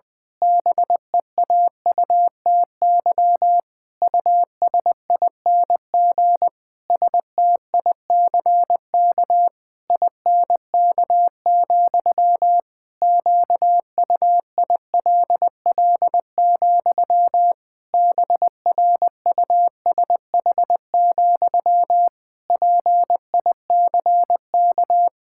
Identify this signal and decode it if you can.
Morse code, heard as a keyed tone, about 20 wpm, 700 Hz. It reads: BEAUTY USING STICK INK, QUILL, BRUSH, PICK
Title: Pangram Challenge v2 - 20wpm